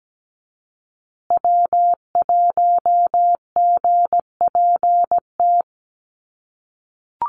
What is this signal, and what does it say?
Morse code, heard as a keyed tone, about 17 wpm, 700 Hz. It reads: W1GPT
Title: USA Call Signs Repeated 3x 17wpm